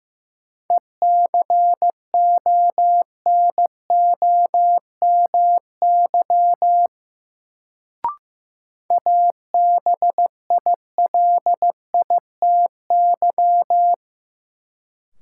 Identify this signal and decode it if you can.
Morse code, heard as a keyed tone, about 16 wpm, 700 Hz. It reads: ECONOMY ABILITY
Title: Sentences from Top 1000 Words - Review Part 3 - New Word Drill - 15wpm